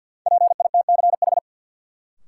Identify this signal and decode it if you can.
Morse code, heard as a keyed tone, about 52 wpm, 700 Hz. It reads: PITCH
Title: Sentences from Top 1300 Words - New Word Drill - 50wpm